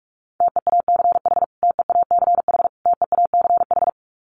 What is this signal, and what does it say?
Morse code, heard as a keyed tone, about 44 wpm, 700 Hz. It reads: TEACH TEACH TEACH
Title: Top 500 Words Repeated 45wpm